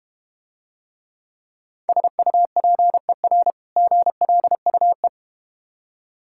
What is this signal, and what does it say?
Morse code, heard as a keyed tone, about 32 wpm, 700 Hz. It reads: SUPER GLUE